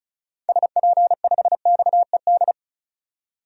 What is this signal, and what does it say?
Morse code, heard as a keyed tone, about 34 wpm, 700 Hz. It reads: SP5XED
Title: DXCC-100 Call Signs - 6 Characters - 35wpm